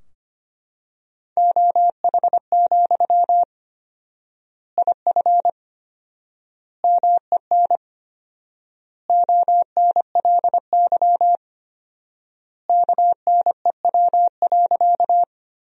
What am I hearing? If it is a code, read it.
Morse code, heard as a keyed tone, about 25 wpm, 700 Hz. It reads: OH, IF MEN ONLY KNEW.
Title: 4x Word-Spacing - Sentences from Top 500 Words - 25wpm